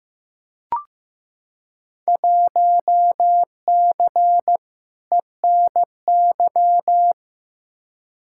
Morse code, heard as a keyed tone, about 15 wpm, 700 Hz. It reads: 1C ENY
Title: Field Day Exchanges Encore - 15wpm